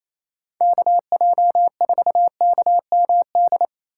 Morse code, heard as a keyed tone, about 27 wpm, 700 Hz. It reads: KJ4KMD
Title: USA Call Signs - 6 Characters - 3x Character Spacing - 28wpm